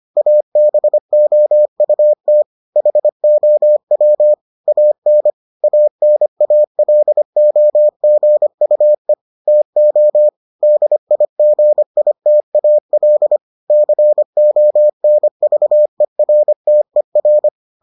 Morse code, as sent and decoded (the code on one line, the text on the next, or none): .- -... --- ..- - / .... --- .-- / .- -. / .- -. .- .-.. --- --. ..- . / - --- / -.. .. --. .. - .- .-.. / -.-. --- -. ...- . .-. - . .-.
ABOUT HOW AN ANALOGUE TO DIGITAL CONVERTER